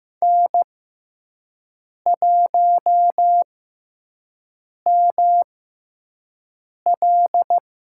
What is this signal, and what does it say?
Morse code, heard as a keyed tone, about 15 wpm, spaced as for 5 wpm, 700 Hz. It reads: N1ML
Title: USA Call Signs - 5 Characters - 6x Character Spacing - 15wpm